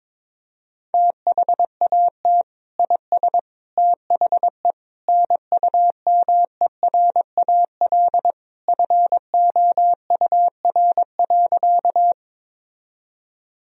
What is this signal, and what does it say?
Morse code, heard as a keyed tone, about 22 wpm, 700 Hz. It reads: THAT IS THE NUMERAL FOUR.
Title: Sentences from Top 1000 Words - Review Part 2 - 22wpm